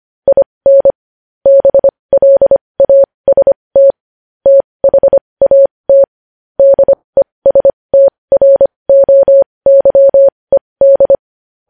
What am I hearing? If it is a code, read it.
Morse code, heard as a keyed tone, about 25 wpm, spaced as for 20 wpm, 550 Hz. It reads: IN BLAST THAT DESTROYED
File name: podcast / News headlines-1x-25x20